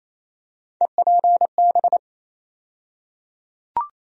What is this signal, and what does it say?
Morse code, heard as a keyed tone, about 28 wpm, 700 Hz. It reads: EPB
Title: USA Call Sign Postfixes - 3 Letters 28wpm